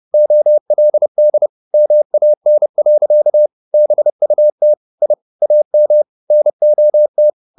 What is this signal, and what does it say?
Morse code, heard as a keyed tone, about 30 wpm, 600 Hz. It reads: OLD MAN. BUT I AM NOT